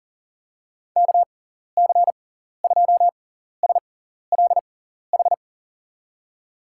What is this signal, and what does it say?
Morse code, heard as a keyed tone, about 40 wpm, spaced as for 14 wpm, 700 Hz. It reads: KC2SLH
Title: USA Call Signs - 6 Characters - 6x Character Spacing - 40wpm